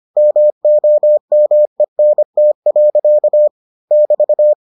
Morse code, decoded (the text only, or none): MOMENT. <BT>